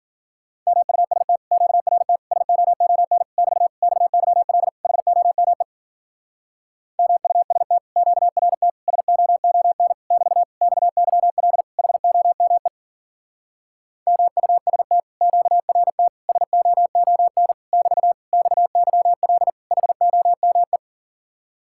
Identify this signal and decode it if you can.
Morse code, heard as a keyed tone, about 48 wpm, 700 Hz. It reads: MUST QRT SOON <BT> XYL HOME MUST QRT SOON <BT> XYL HOME MUST QRT SOON <BT> XYL HOME
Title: Speed-Racing Common QSO Phrases Encore - 40wpm